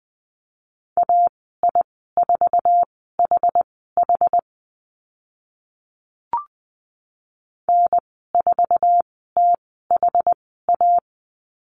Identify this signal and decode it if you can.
Morse code, heard as a keyed tone, about 20 wpm, spaced as for 15 wpm, 700 Hz. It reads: AI4HH N4THA
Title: 062 Lesson 11g - Letter h - Call Signs 20wpm